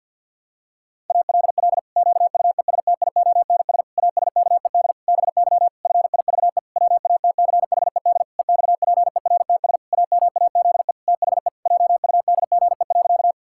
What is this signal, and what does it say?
Morse code, heard as a keyed tone, about 50 wpm, 700 Hz. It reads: ALL QUESTIONS ASKED BY FIVE WATCHED EXPERTS AMAZE THE JUDGE.
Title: Pangram Challenge v2 - 50wpm